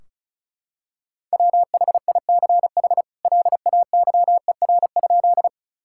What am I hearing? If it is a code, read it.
Morse code, heard as a keyed tone, about 36 wpm, 700 Hz. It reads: WHICH LAYER?
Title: Sentences from Top 1600 Words 35wpm